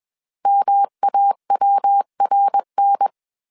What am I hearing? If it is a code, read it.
Morse code, heard as a keyed tone, about 21 wpm, 800 Hz. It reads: MAWRN